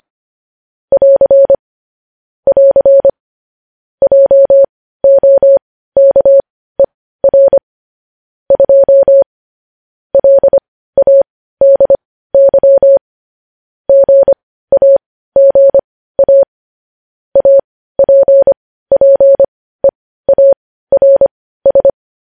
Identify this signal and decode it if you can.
Morse code, heard as a keyed tone, about 25 wpm, spaced as for 15 wpm, 550 Hz. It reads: <AR> <AR> JOKER 2 LADY GAGA APPEARS